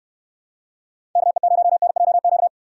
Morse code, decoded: KE0NJQ

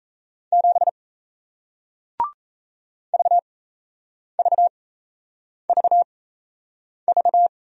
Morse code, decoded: 7 VVVV

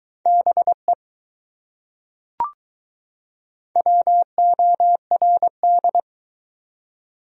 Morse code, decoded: BE WORD

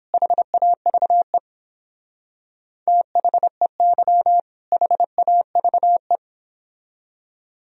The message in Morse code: HAVE THEY HAVE

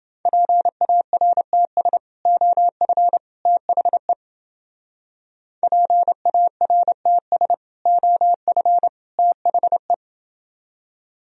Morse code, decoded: PARTS OF THE PARTS OF THE

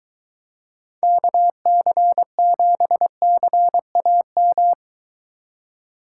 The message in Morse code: KC7CAM

